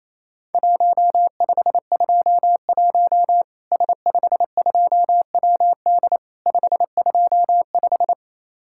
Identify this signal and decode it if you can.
Morse code, heard as a keyed tone, about 28 wpm, 700 Hz. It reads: 1521 S52WD 525
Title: Worked All Europe DX Contest - QTC Exchange - 28wpm